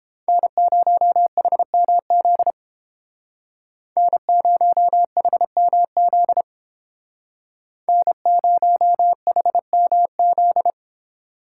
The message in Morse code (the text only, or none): N0HMZ N0HMZ N0HMZ